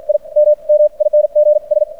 Morse code, heard as a keyed tone, about 56 wpm, spaced as for 40 wpm, 600 Hz. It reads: IMMINGS
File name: archived / CRSnet